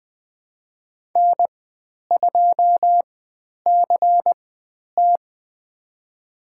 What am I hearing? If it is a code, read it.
Morse code, heard as a keyed tone, about 20 wpm, spaced as for 10 wpm, 700 Hz. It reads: N2CT
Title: 095 Lesson 16g - Letter c - Call Signs 20wpm